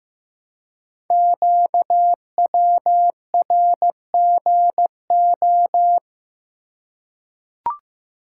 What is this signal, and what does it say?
Morse code, heard as a keyed tone, about 16 wpm, 700 Hz. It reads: QWRGO
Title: Code Groups - 5 Letters - 15wpm